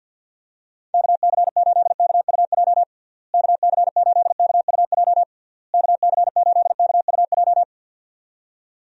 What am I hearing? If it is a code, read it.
Morse code, heard as a keyed tone, about 51 wpm, 700 Hz. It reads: KC8KUJ KC8KUJ KC8KUJ
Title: USA Call Signs - 6 Characters - Repeated 3x - 50wpm